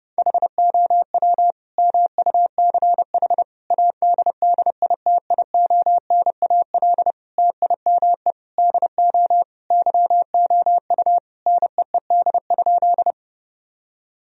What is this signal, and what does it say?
Morse code, heard as a keyed tone, about 32 wpm, 700 Hz. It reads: HOW MUCH ADDITIONAL TIME DO YOU NEED?
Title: Sentences from Top 1000 Words - Review Part 4 - 30wpm